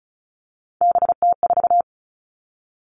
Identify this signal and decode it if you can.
Morse code, heard as a keyed tone, about 34 wpm, 700 Hz. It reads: BT4
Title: All Cut Numbers - 3-Digit 35wpm